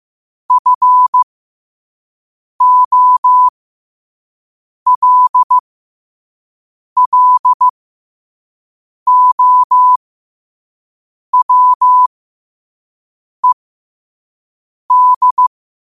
Morse code, heard as a keyed tone, about 15 wpm, spaced as for 5 wpm, 1000 Hz. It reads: FOLLOWED
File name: qotdfiles / 5wpm